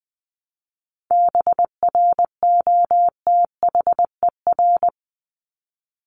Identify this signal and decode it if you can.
Morse code, heard as a keyed tone, about 20 wpm, 700 Hz. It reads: BROTHER
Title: Top 1751-2000 Words 20wpm